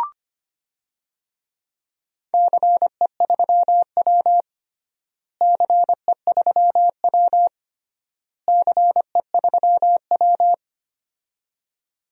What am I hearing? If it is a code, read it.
Morse code, heard as a keyed tone, about 25 wpm, 700 Hz. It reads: CE3W CE3W CE3W
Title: DXCC-100 Call Signs - 4 Characters - Repeated 3x - 25wpm